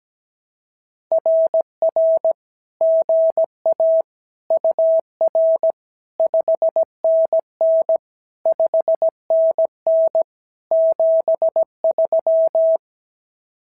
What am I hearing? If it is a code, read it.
Morse code, heard as a keyed tone, about 16 wpm, 650 Hz. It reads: RR GA UR 5NN 5NN 73
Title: SOTA QSO Practice 17wpm